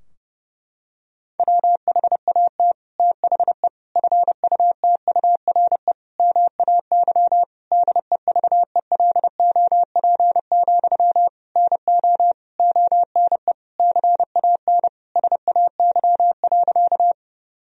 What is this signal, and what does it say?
Morse code, heard as a keyed tone, about 30 wpm, 700 Hz. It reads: WHAT THE FUTURE MAY DEVELOP, NO ONE CAN SAY.
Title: Sentences from Top 1000 Words - Review Part 2 - 30wpm